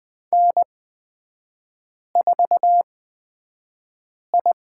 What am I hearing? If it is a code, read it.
Morse code, heard as a keyed tone, about 20 wpm, spaced as for 5 wpm, 700 Hz. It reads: N4I
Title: 250 Lesson 40e - BK - Three Characters 20wpm